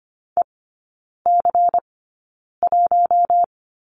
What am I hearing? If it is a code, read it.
Morse code, heard as a keyed tone, about 25 wpm, spaced as for 9 wpm, 700 Hz. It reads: EC1
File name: course / 25:10wpm / 093 Lesson 16e - Letter c - Three Characters 25wpm